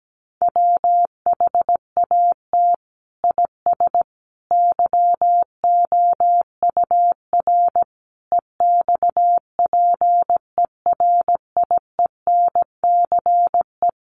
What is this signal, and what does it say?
Morse code, heard as a keyed tone, about 17 wpm, 700 Hz. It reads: WHAT IS YOUR EXPERIENCE
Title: Sentences from Top 600 Words 17wpm